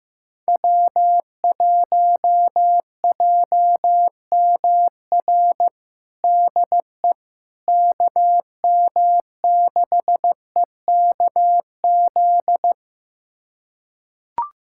Morse code, heard as a keyed tone, about 15 wpm, 700 Hz. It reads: W1JMR DE KM6EKZ
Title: USA Callsign DE Callsign 15wpm